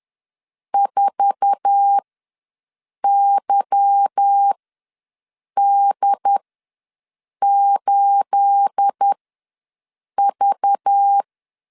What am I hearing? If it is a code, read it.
Morse code, heard as a keyed tone, about 10 wpm, spaced as for 6 wpm, 800 Hz. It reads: 4YD8V
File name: LEZIONI / 9 28 ws 10wpm 4124